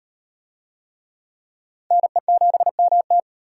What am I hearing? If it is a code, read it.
Morse code, heard as a keyed tone, about 38 wpm, 700 Hz. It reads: NE7MT